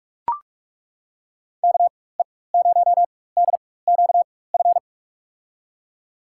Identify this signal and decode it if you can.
Morse code, heard as a keyed tone, about 43 wpm, spaced as for 21 wpm, 700 Hz. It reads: KE0DQF
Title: USA Call Signs - 6 Characters - 4x Character Spacing - 45wpm